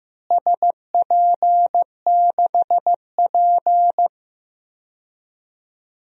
Morse code, heard as a keyed tone, about 15 wpm, 700 Hz. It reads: SP6P
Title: DXCC-100 Call Signs - 4 Characters - Repeated 2x - 15wpm